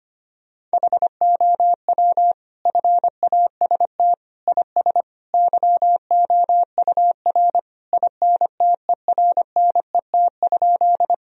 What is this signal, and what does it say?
Morse code, heard as a keyed tone, about 25 wpm, 700 Hz. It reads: HOW FAST IS YOUR INTERNET?
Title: Sentences from Top 1000 Words - Review Part 4 - 25wpm